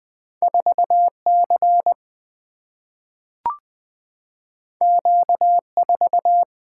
Morse code, heard as a keyed tone, about 20 wpm, 700 Hz. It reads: 4C Q4
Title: Single-Letter Single-Number 20wpm